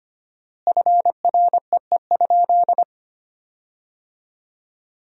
Morse code, heard as a keyed tone, about 25 wpm, 700 Hz. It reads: FREE?
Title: 4x Word-Spacing - Sentences from Top 500 Words - 25wpm